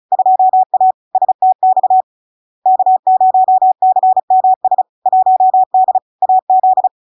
Morse code, heard as a keyed tone, about 35 wpm, 750 Hz. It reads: 2A STX K0CMS 1D AZ
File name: LARK-FD-2022-Log-in-CW-35wpm-750-Hz